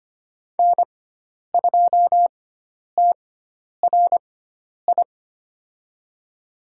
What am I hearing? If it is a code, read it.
Morse code, heard as a keyed tone, about 25 wpm, spaced as for 10 wpm, 700 Hz. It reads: N2TRI
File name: Lesson 16g - Letter c - Call Signs 25wpm